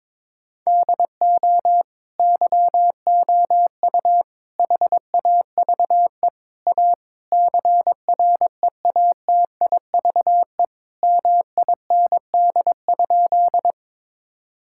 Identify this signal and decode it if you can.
Morse code, heard as a keyed tone, about 22 wpm, 700 Hz. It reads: DO YOU HAVE A CREATIVE MIND?
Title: Sentences from Top 1000 Words - Review Part 5 - 22wpm